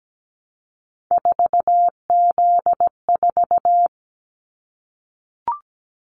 Morse code, heard as a keyed tone, about 17 wpm, 700 Hz. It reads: 4Z4
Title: Callsign Prefixes for Top 50 DXCC - 17wpm